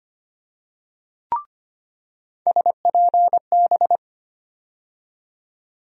Morse code, heard as a keyed tone, about 25 wpm, 700 Hz. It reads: SPB
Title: USA Call Sign Postfixes - 3 Letters 25wpm